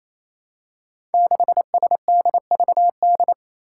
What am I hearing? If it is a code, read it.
Morse code, heard as a keyed tone, about 28 wpm, 700 Hz. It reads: 6SDVD
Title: Five Challenging Letter-Number Combinations 28wpm